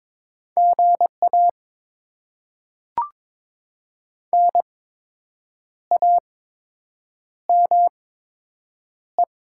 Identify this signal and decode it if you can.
Morse code, heard as a keyed tone, about 22 wpm, spaced as for 7 wpm, 700 Hz. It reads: GA NAME
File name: Top 20 QSO Elements - 8x Character Spacing - 22wpm